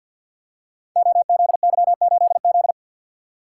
Morse code, heard as a keyed tone, about 47 wpm, 700 Hz. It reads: OZY8Z
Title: Code Groups - 5 Letter:Numbers - 50wpm